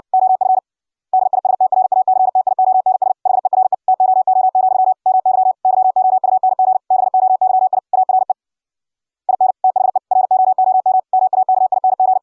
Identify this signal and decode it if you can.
Morse code, heard as a keyed tone, about 60 wpm, 750 Hz. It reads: OF DISTRACTIONS HERE TOO, MY YOUNG BOYI ARE IN THE ROOM MAKING